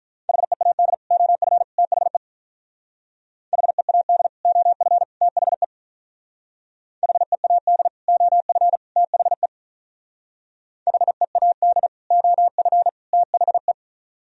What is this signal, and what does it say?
Morse code, heard as a keyed tone, about 46 wpm, 700 Hz. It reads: HEAD OF THE HEAD OF THE HEAD OF THE HEAD OF THE